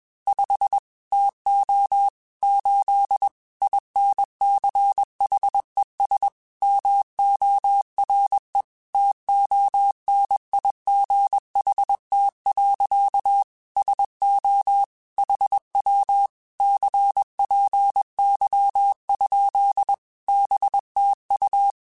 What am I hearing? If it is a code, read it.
Morse code, heard as a keyed tone, about 21 wpm, 800 Hz. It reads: 5 TO 8 INCHES MORE TONIGHT. SO HW CPY? BTU
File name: QSOs / QSOs-mp3s / WA7QOX 1@20